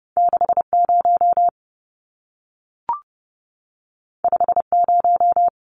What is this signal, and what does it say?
Morse code, heard as a keyed tone, about 30 wpm, 700 Hz. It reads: 60 50
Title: Numbers - 2-Digit 30wpm